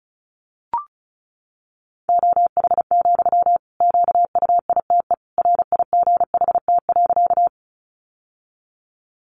Morse code, heard as a keyed tone, about 34 wpm, 700 Hz. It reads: OH, QUITE RIGHT.
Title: Sentences from Top 700 Words 35wpm